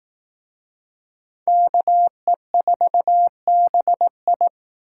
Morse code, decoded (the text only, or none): KE4BI